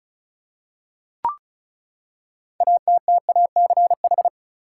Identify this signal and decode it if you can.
Morse code, heard as a keyed tone, about 36 wpm, 700 Hz. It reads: ATTACH
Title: Six Letter Words - 35wpm